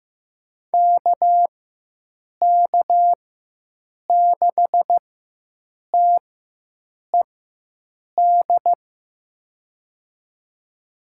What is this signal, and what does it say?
Morse code, heard as a keyed tone, about 15 wpm, spaced as for 7 wpm, 700 Hz. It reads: KK6TED
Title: USA Call Signs - 6 Characters - 4x Character Spacing - 15wpm